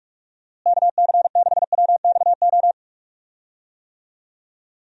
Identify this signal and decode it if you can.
Morse code, heard as a keyed tone, about 43 wpm, 700 Hz. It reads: KC6WXO
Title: USA Call Signs - 6 Characters - Repeated 2x - 45wpm